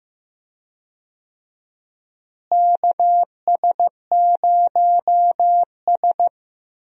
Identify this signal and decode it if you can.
Morse code, heard as a keyed tone, about 15 wpm, 700 Hz. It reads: KS0S
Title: USA Call Signs - 4 Characters - Repeated 2x - 15wpm